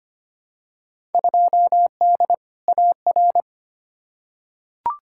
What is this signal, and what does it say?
Morse code, heard as a keyed tone, about 25 wpm, 700 Hz. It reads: 2D AR
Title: Field Day Exchanges Encore - 25wpm